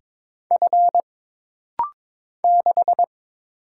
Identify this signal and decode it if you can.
Morse code, heard as a keyed tone, about 22 wpm, 700 Hz. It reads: F 6
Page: This rapid-fire timing is geared to push your ICR (Instant Character Recognition) skills.